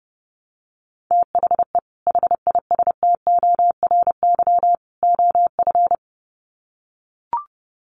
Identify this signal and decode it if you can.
Morse code, heard as a keyed tone, about 30 wpm, 700 Hz. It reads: THE HISTORY OF
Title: Sets of 3 Words 30wpm